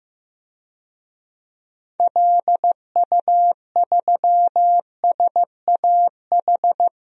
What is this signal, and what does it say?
Morse code, heard as a keyed tone, about 15 wpm, 700 Hz. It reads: LU3SAH
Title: DXCC-100 Call Signs - 6 Characters - 15wpm